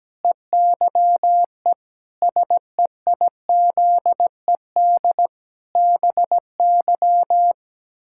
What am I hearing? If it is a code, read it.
Morse code, heard as a keyed tone, about 17 wpm, 700 Hz. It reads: EYE SEIZED BY
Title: Pangram Challenge v2 - 17wpm